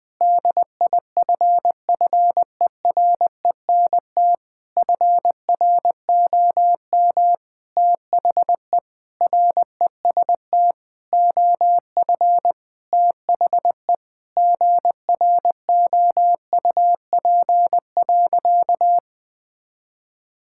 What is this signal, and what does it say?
Morse code, heard as a keyed tone, about 21 wpm, 700 Hz. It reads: DIFFERENT FROM THE REST OF THE GROUP.
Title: Sentences with 5-Word Phrases - Spoken Beginning - 20wpm